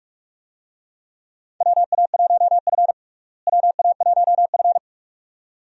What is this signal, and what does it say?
Morse code, heard as a keyed tone, about 42 wpm, 700 Hz. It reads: WA1F WA1F
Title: USA Call Signs - 4 Characters - Repeated 2x - 45wpm